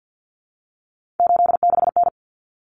Morse code, heard as a keyed tone, about 50 wpm, 700 Hz. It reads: 86N
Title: Common Cut Numbers - 3-Digit 50wpm